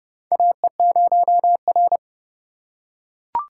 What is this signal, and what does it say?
Morse code, heard as a keyed tone, about 30 wpm, 700 Hz. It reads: AE0R
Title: USA Call Signs - 4 Characters - 2x Character Spacing - 30wpm